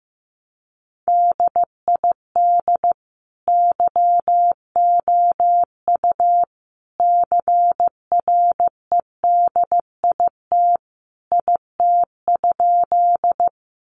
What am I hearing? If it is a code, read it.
Morse code, heard as a keyed tone, about 15 wpm, 700 Hz. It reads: DID YOU CREDIT IT?
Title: Sentences from Top 1000 Words 15wpm